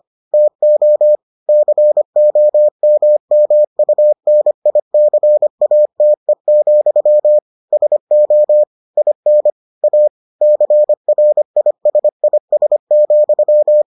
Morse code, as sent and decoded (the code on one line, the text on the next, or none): - --- / -.-. --- -- -- ..- -. .. -.-. .- - . --..-- / ... --- / .. -. / .- / -.-. .-. .. ... .. ... --..--
TO COMMUNICATE, SO IN A CRISIS,